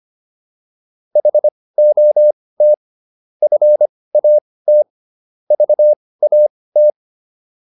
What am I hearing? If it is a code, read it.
Morse code, heard as a keyed tone, about 25 wpm, spaced as for 18 wpm, 600 Hz. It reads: HOT FAT VAT